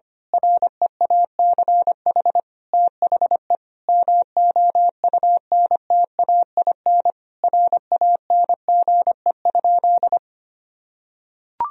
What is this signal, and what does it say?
Morse code, heard as a keyed tone, about 25 wpm, 690 Hz. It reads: REACH THE MOUNTAIN RANGE?
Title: Sentences from Top 1000 Words - Review Part 4 - 25wpm